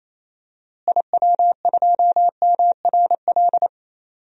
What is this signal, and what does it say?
Morse code, heard as a keyed tone, about 28 wpm, 700 Hz. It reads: IW2MRL
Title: DXCC-100 Call Signs - 6 Characters - Repeated 3x - 28wpm